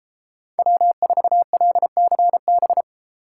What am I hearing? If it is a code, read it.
Morse code, heard as a keyed tone, about 34 wpm, 700 Hz. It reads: W4LCB